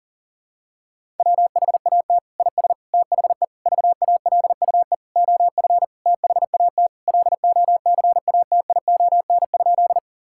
Morse code, heard as a keyed tone, about 40 wpm, 700 Hz. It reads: WHAT IS THE VALUE OF THAT LOCATION?
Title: Sentences from Top 1000 Words - Review Part 6 - 40wpm